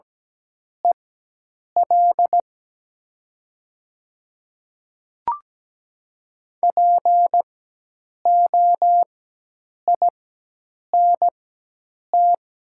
Morse code, heard as a keyed tone, about 17 wpm, spaced as for 8 wpm, 700 Hz. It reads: EL POINT
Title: 4x Character Spacing - Top 500 Words - 17wpm